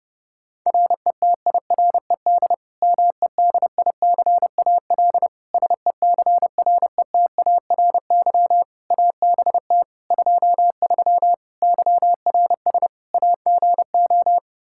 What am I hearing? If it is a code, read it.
Morse code, heard as a keyed tone, about 30 wpm, 700 Hz. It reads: RETIRED MEDICAL SECRETARY ABT 23 YRS AGO
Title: K1USN SST QSO Practice 30wpm